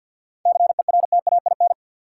Morse code, heard as a keyed tone, about 53 wpm, 700 Hz. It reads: CERTAIN
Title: Sentences from Top 500 Words - New Word Drill - 50wpm